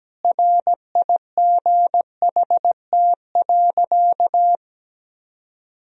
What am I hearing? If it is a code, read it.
Morse code, heard as a keyed tone, about 17 wpm, 700 Hz. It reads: RIGHT.
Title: Sentences from Top 1000 Words - Review Part 4 - 17wpm